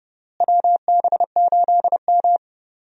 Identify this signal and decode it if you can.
Morse code, heard as a keyed tone, about 30 wpm, 700 Hz. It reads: WB8M
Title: USA Call Signs - 4 Characters - Repeated 3x - 30wpm